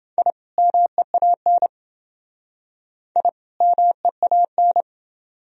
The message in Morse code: I MEAN I MEAN

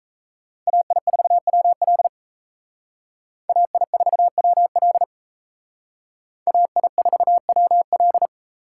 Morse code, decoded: AI4WL AI4WL AI4WL

